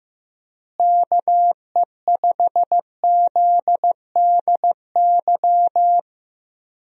Morse code, heard as a keyed tone, about 15 wpm, 700 Hz. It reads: KE5ZDY
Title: USA Call Signs - 6 Characters - Repeated 3x - 15wpm